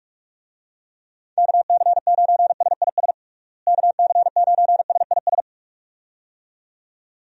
Morse code, heard as a keyed tone, about 46 wpm, 700 Hz. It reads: KC9SIS KC9SIS
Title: USA Call Signs - 6 Characters - Repeated 2x - 45wpm